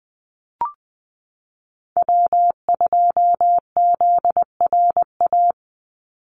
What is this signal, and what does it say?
Morse code, heard as a keyed tone, about 20 wpm, 700 Hz. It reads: W2ZRA
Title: Call Signs - 20wpm